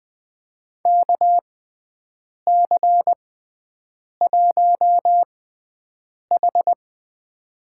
Morse code, heard as a keyed tone, about 20 wpm, spaced as for 7 wpm, 700 Hz. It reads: KC1H
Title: USA Call Signs - 6x Character Spacing - 20wpm